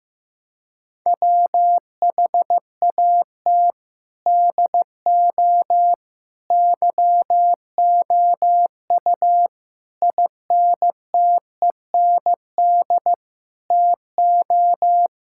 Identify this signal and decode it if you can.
Morse code, heard as a keyed tone, about 15 wpm, 700 Hz. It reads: WHAT DO YOU INTEND TO
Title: Sentences from Top 1600 Words 15wpm